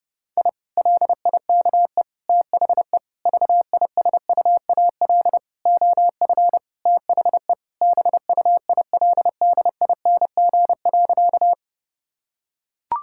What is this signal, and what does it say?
Morse code, heard as a keyed tone, about 30 wpm, 700 Hz. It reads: I LIKE THE VISUAL OF THE BUILDING.
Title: Sentences from Top 1000 Words - Review Part 6 - 30wpm